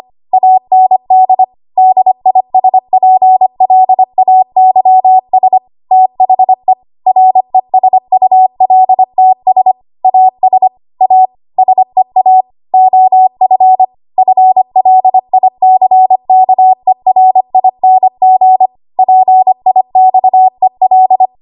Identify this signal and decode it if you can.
Morse code, heard as a keyed tone, about 25 wpm, 750 Hz. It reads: AND DISPLAYS THE RESULTS AS A SEA OF FLICKERING PIXEL
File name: Archive / 25 WPM / 25WPM